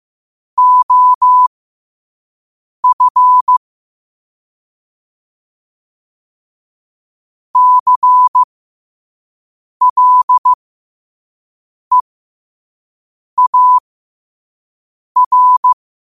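Morse code, heard as a keyed tone, about 15 wpm, spaced as for 5 wpm, 1000 Hz. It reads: OF CLEAR
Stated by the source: Morse code sent at 5 wpm.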